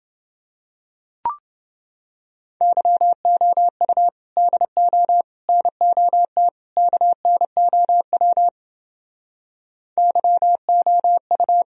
Morse code, heard as a keyed tone, about 29 wpm, 700 Hz. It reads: YOU DO NOT KNOW YOU